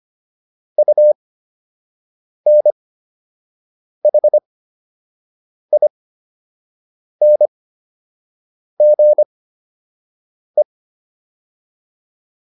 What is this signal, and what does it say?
Morse code, heard as a keyed tone, about 25 wpm, spaced as for 6 wpm, 600 Hz. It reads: UNHINGE